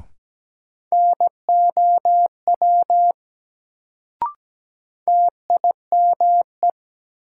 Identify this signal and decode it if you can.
Morse code, heard as a keyed tone, about 17 wpm, 700 Hz. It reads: NOW TIME
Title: Sentences from Top 100 Words - New Word Drill - 17wpm